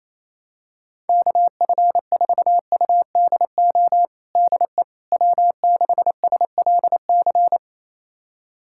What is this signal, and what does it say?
Morse code, heard as a keyed tone, about 28 wpm, 700 Hz. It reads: KF4UDO DE W6SLC
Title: USA Callsign DE Callsign 28wpm